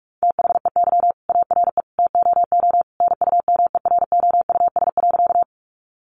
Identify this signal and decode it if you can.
Morse code, heard as a keyed tone, about 44 wpm, 700 Hz. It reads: THEY ARE TOO NUMEROUS.
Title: Sentences from Top 1400 Words 45wpm